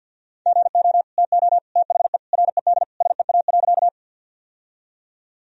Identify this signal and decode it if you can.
Morse code, heard as a keyed tone, about 52 wpm, 700 Hz. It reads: GO TO THE RED SEA.
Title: Sentences from Top 1000 Words - Review Part 2 - 50wpm